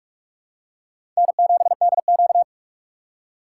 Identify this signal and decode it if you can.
Morse code, heard as a keyed tone, about 42 wpm, 700 Hz. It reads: N7DQ
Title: USA Call Signs - 4 Characters - 2x Character Spacing - 45wpm